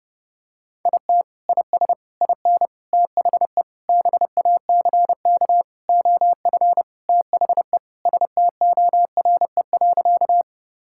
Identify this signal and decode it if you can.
Morse code, heard as a keyed tone, about 32 wpm, 700 Hz. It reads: IT IS IN THE BACK OF THE STORE.
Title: Sentences from Top 1000 Words - Review Part 5 - 30wpm